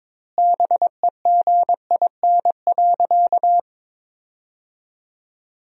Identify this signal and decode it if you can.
Morse code, heard as a keyed tone, about 22 wpm, 700 Hz. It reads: BEGIN.
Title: 2x Word-Spacing - Sentences from Top 300 Words - 22wpm